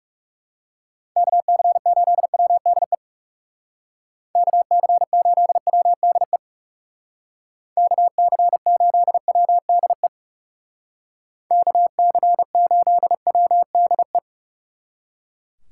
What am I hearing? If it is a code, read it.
Morse code, heard as a keyed tone, about 36 wpm, 700 Hz. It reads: KC8WDE KC8WDE KC8WDE KC8WDE
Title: Speed Racing - USA 2x3 Call Signs 30wpm